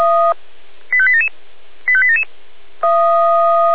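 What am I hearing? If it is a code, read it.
Five-tone selective call (ZVEI), heard as quick runs of five tones, tones of about 70 ms. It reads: 85877 85877
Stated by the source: ZVEI five-tone call received by SDR.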